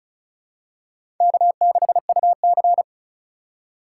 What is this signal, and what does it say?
Morse code, heard as a keyed tone, about 36 wpm, 700 Hz. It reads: K6UC
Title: USA Call Signs - 4 Characters - 3x Character Spacing - 35wpm